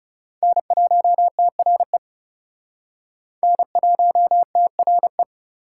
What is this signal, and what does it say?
Morse code, heard as a keyed tone, about 40 wpm, 700 Hz. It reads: N1TRE N1TRE